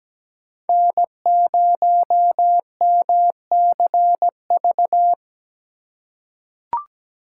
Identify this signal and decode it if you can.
Morse code, heard as a keyed tone, about 17 wpm, 700 Hz. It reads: N0MCV
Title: USA Call Signs - 5 Characters - 2x Character Spacing - 17wpm